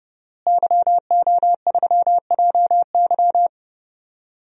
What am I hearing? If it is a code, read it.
Morse code, heard as a keyed tone, about 31 wpm, 700 Hz. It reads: YO3JY
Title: DXCC-100 Call Signs - 5 Characters - Repeated 3x - 30wpm